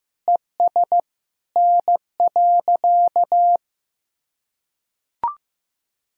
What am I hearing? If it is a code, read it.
Morse code, heard as a keyed tone, about 16 wpm, 700 Hz. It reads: ES N.
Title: Sentences from Top 500 Words and Abbreviations 15wpm